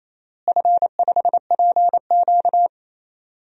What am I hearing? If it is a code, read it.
Morse code, heard as a keyed tone, about 28 wpm, 700 Hz. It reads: F5PQ
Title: DXCC-100 Call Signs - 4 Characters - Repeated 2x - 28wpm